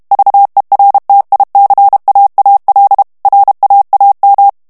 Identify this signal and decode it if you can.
Morse code, heard as a keyed tone, about 32 wpm, 800 Hz. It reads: VERTICAAL RAAM
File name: morse mp3 / HOLLANDAIS texte clair 1 vitesse 30 mots minute